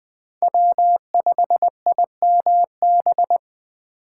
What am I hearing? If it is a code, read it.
Morse code, heard as a keyed tone, about 21 wpm, 700 Hz. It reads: W5IMB